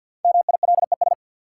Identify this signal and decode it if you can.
Morse code, heard as a keyed tone, about 52 wpm, 700 Hz. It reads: MILES
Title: Sentences from Top 1000 Words - Review Part 3 - New Word Drill - 50wpm